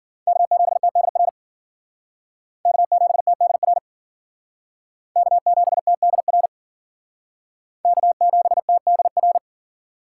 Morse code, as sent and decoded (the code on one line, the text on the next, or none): -.- --... - -.. .-. / -.- --... - -.. .-. / -.- --... - -.. .-. / -.- --... - -.. .-.
K7TDR K7TDR K7TDR K7TDR